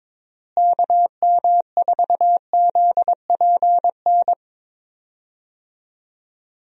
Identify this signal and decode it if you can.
Morse code, heard as a keyed tone, about 22 wpm, 700 Hz. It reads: KM4ZPN
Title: USA Call Signs - 6 Characters - Repeated 2x - 22wpm